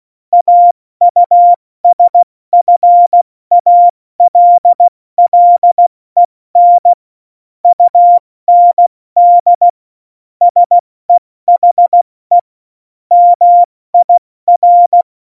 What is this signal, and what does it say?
Morse code, heard as a keyed tone, about 16 wpm, 700 Hz. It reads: AUSFALLEN UND SEHE MIR
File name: Morsekurs / Witze / Witz 29. Mai 2018, 70 F80